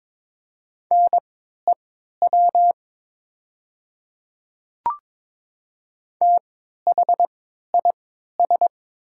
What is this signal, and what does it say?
Morse code, heard as a keyed tone, about 22 wpm, spaced as for 13 wpm, 690 Hz. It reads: NEW THIS
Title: 3x Character Spacing - Top 200 Words - 22wpm